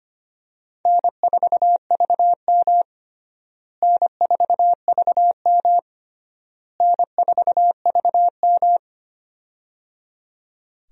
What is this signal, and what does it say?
Morse code, heard as a keyed tone, about 25 wpm, 700 Hz. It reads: N4VM N4VM N4VM
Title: USA Call Signs - 4 Characters - Repeated 3x - 25wpm